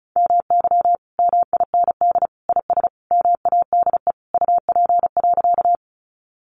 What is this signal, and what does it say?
Morse code, heard as a keyed tone, about 34 wpm, 700 Hz. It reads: MY MIND IS MADE UP.
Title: Sentences from Top 500 Words 35wpm